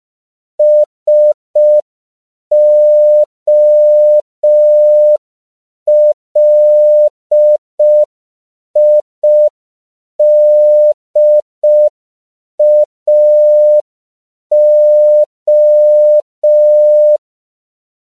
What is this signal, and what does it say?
Morse code, heard as a keyed tone, about 5 wpm, 600 Hz. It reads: SOLIDAO
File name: CW / 5PPM